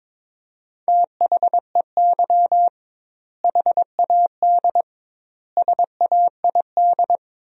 Morse code, heard as a keyed tone, about 22 wpm, 700 Hz. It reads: THEY HAD SAID
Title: 2x Word-Spacing - Sentences from Top 400 Words - 22wpm